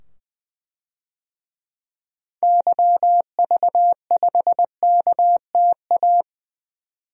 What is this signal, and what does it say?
Morse code, heard as a keyed tone, about 20 wpm, 700 Hz. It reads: YV5KTA
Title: DXCC-100 Call Signs - 6 Characters - 20wpm